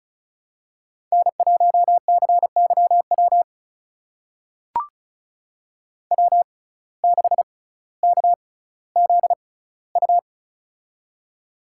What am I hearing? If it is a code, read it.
Morse code, heard as a keyed tone, about 36 wpm, spaced as for 18 wpm, 700 Hz. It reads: N1CYW W6KZU
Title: USA Call Signs - 5 Characters - 6x Character Spacing - 35wpm